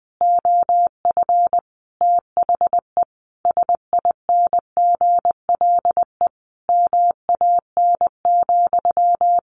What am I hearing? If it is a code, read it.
Morse code, heard as a keyed tone, about 20 wpm, 700 Hz. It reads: OF THE SINGLE MAN,
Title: Sentences from Top 1400 Words 20wpm